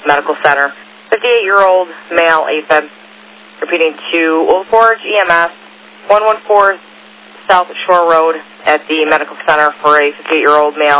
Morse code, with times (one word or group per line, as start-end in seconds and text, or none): none